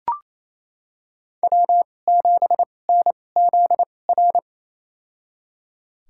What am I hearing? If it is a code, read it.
Morse code, heard as a keyed tone, about 28 wpm, spaced as for 20 wpm, 700 Hz. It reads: W7NZR
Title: USA Call Signs - 5 Characters - 2x Character Spacing - 28wpm